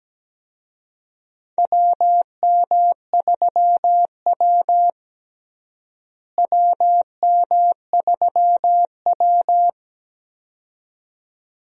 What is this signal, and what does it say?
Morse code, heard as a keyed tone, about 17 wpm, 700 Hz. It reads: WM3W WM3W
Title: USA Call Signs - 4 Characters - Repeated 2x - 17wpm